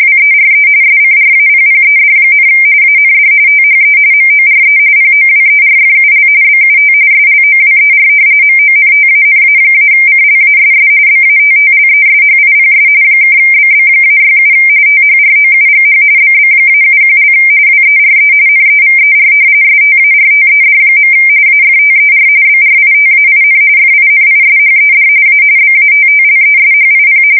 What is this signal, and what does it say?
RTTY, about 45.45 baud. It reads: WERE SCRAPPED, AND THE MORE POWERFUL BUT SLOWER MONTANA-CLASS BATTLESHIPS (BB-67 THROUGH BB-71) WERE NEVER STARTED. AFTER BRITAIN FINISHED A BATTLESHIP
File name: fsk170 / battleshipnj